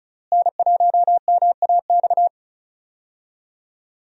Morse code, heard as a keyed tone, about 36 wpm, 700 Hz. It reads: N1MAX
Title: USA Call Signs - 5 Characters - Repeated 3x - 35wpm